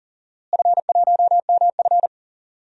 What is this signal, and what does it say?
Morse code, heard as a keyed tone, about 40 wpm, 700 Hz. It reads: F1MF
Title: DXCC-100 Call Signs - 4 Characters - Repeated 3x - 40wpm